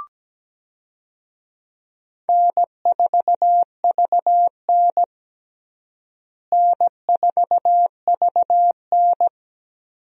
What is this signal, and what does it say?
Morse code, heard as a keyed tone, about 17 wpm, 700 Hz. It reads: N4VN N4VN
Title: USA Call Signs - 4 Characters - Repeated 2x - 17wpm